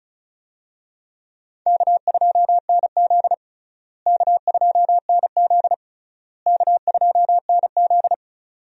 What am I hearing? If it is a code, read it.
Morse code, heard as a keyed tone, about 36 wpm, 700 Hz. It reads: K2NZ K2NZ K2NZ
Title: USA Call Signs - 4 Characters - Repeated 3x - 35wpm